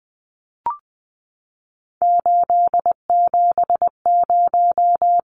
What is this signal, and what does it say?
Morse code, heard as a keyed tone, about 20 wpm, 700 Hz. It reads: 870
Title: Numbers - 3-Digit 20wpm